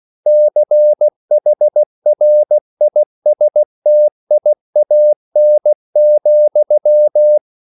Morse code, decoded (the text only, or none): CHRISTIAN,